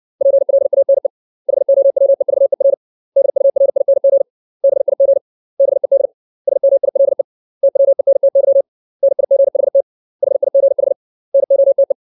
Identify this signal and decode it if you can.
Morse code, heard as a keyed tone, about 60 wpm, 550 Hz. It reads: PLANE HOWEVER DURING BIG BED SMILE TWENTY NIGHT HIGH NOTE
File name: 632 words / 632 Words 60WPM